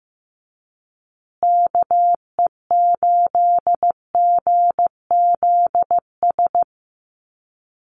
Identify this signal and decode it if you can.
Morse code, heard as a keyed tone, about 15 wpm, 700 Hz. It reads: KE8GZS
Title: USA Call Signs Repeated 3x 15wpm